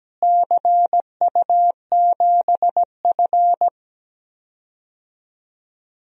Morse code, heard as a keyed tone, about 17 wpm, 700 Hz. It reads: CU7F
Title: DXCC-100 Call Signs - 4 Characters - Repeated 2x - 17wpm